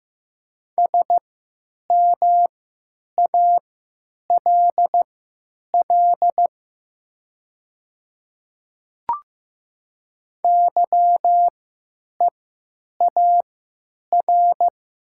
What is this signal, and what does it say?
Morse code, heard as a keyed tone, about 15 wpm, spaced as for 9 wpm, 700 Hz. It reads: SMALL YEAR
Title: 3x Character Spacing - Top 300 Words - 15wpm